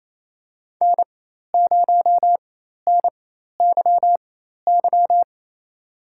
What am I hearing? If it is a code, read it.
Morse code, heard as a keyed tone, about 28 wpm, spaced as for 13 wpm, 700 Hz. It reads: N0NYY
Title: USA Call Signs - 5 Characters - 4x Character Spacing - 28wpm